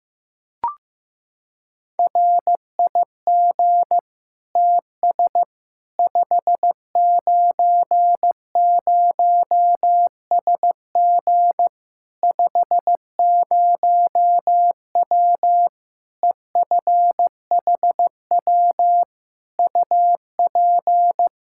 Morse code, heard as a keyed tone, about 15 wpm, 700 Hz. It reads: RIG TS 590SG 50W EFHW UP
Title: Practice QSO - Exchange 2 - Info - 15wpm